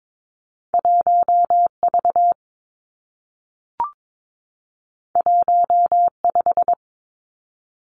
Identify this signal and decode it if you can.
Morse code, heard as a keyed tone, about 22 wpm, 700 Hz. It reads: 1 V 1 5
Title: Two Challenging Letter-Number Combinations 22wpm